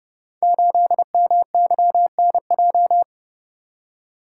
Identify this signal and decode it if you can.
Morse code, heard as a keyed tone, about 29 wpm, 700 Hz. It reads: 8MYNJ
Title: Code Groups - 5 Letter:Numbers - 30wpm